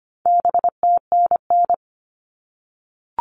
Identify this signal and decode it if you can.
Morse code, heard as a keyed tone, about 25 wpm, 700 Hz. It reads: BTNN
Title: All Cut Numbers - 4-Digit 25wpm